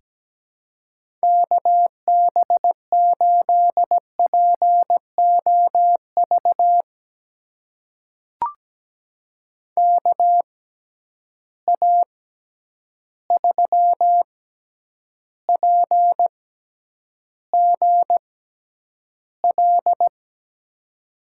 Morse code, heard as a keyed tone, about 17 wpm, spaced as for 9 wpm, 700 Hz. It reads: KB8POV KA3PGL
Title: USA Call Signs - 6 Characters - 6x Character Spacing - 17wpm